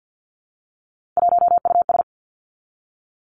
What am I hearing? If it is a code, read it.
Morse code, heard as a keyed tone, about 50 wpm, 700 Hz. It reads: 1US
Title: Three Challenging Letter-Number Combinations 50wpm